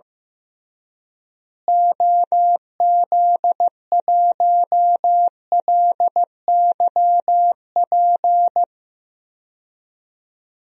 Morse code, heard as a keyed tone, about 15 wpm, 700 Hz. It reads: OZ1LYP
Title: DXCC-100 Call Signs - 6 Characters - Repeated 3x - 15wpm